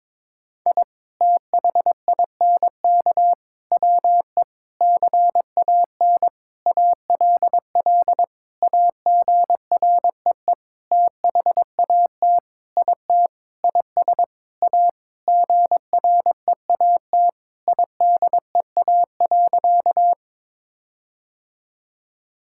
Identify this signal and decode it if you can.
Morse code, heard as a keyed tone, about 22 wpm, 700 Hz. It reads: I THINK WE CAN ALL AGREE THAT IT IS A GREAT IDEA.
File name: Sentences with 7-Word Phrases - 22wpm